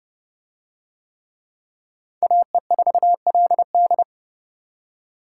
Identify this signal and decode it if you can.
Morse code, heard as a keyed tone, about 30 wpm, 700 Hz. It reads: AE4LD